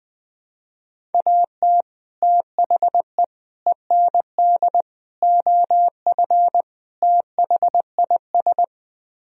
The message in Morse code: AT THE END OF THIS

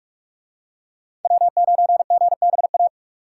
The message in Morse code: W9GBA

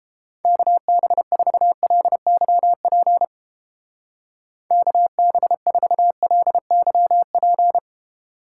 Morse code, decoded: KB4LYP KB4LYP

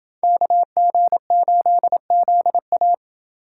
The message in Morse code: KG8ZA

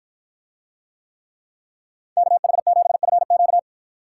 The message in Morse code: KH7FQ

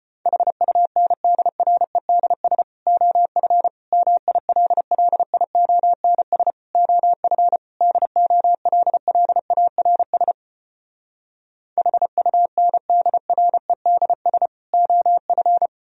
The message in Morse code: HUNDREDS OF MILLIONS OF DOLLARS HUNDREDS OF

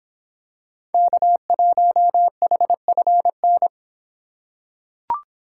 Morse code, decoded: K1HFN

